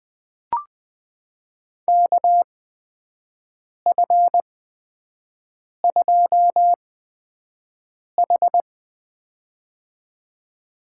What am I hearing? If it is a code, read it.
Morse code, heard as a keyed tone, about 21 wpm, spaced as for 5 wpm, 700 Hz. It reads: KF2H
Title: USA Call Signs - 4 Characters - 8x Character Spacing - 20wpm